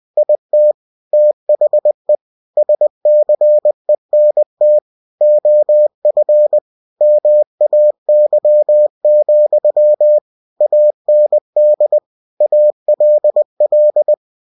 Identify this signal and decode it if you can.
Morse code, heard as a keyed tone, about 20 wpm, 600 Hz. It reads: IT THE SCENT OF MAY, AND ALL
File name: cw-books / winnie-the-pooh / winnie-20wpm-page-0101